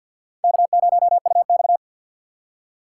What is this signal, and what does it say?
Morse code, heard as a keyed tone, about 52 wpm, 690 Hz. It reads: K0UX
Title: USA Call Signs - 6x Character Spacing - 50wpm